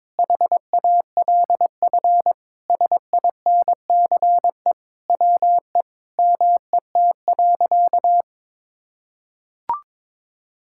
Morse code, heard as a keyed tone, about 22 wpm, 700 Hz. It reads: HALF SINCE WE MET.
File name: practice / 22wpm / Sentences with 4-Word Phrases - 22wpm